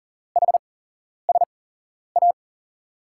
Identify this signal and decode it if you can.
Morse code, heard as a keyed tone, about 38 wpm, spaced as for 11 wpm, 700 Hz. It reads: HSA
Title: USA Call Signs - 5 Characters - 8x Character Spacing - 40wpm